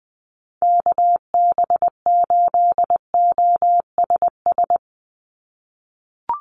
Morse code, heard as a keyed tone, about 20 wpm, 700 Hz. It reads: KB8OSS
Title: USA Call Signs 20wpm